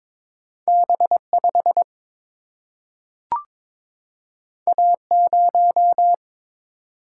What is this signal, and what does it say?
Morse code, heard as a keyed tone, about 22 wpm, 700 Hz. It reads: B 5 A 0
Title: Single-Letter Single-Number 22wpm